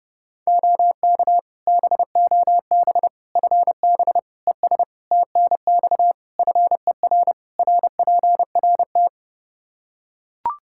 Morse code, heard as a keyed tone, about 30 wpm, 700 Hz. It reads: OK BOB FB ES TNX FER RPRT
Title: Practice QSO - Exchange 2 - Info - 30wpm